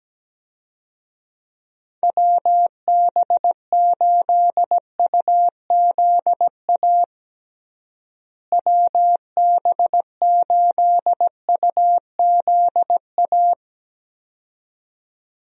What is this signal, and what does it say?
Morse code, heard as a keyed tone, about 17 wpm, 700 Hz. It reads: WB8UZA WB8UZA
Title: USA Call Signs - 6 Characters - Repeated 2x - 17wpm